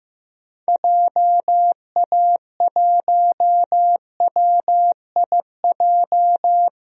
Morse code, heard as a keyed tone, about 15 wpm, 700 Hz. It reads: JA1WIJ
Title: DXCC-100 Call Signs - 6 Characters - Repeated 2x - 15wpm